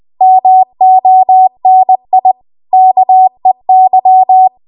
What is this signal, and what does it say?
Morse code, heard as a keyed tone, about 20 wpm, 750 Hz. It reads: MONI KEY
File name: Archive / 20 WPM / 20WPM